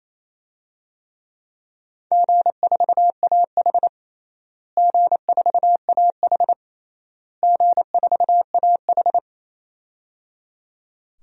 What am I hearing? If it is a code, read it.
Morse code, heard as a keyed tone, about 28 wpm, 700 Hz. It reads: G4AH G4AH G4AH
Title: DXCC-100 Call Signs - 4 Characters - Repeated 3x - 28wpm